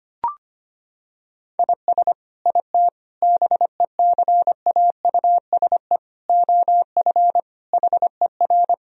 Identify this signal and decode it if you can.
Morse code, heard as a keyed tone, about 25 wpm, 700 Hz. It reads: IS IT BECAUSE OF HER
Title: Sentences from Top 1000 Words - Review Part 3 - 25wpm